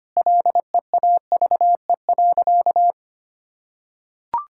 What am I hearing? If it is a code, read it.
Morse code, heard as a keyed tone, about 25 wpm, 700 Hz. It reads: LEAVE.
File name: practice / 25wpm / Sentences with 4-Word Phrases - 25wpm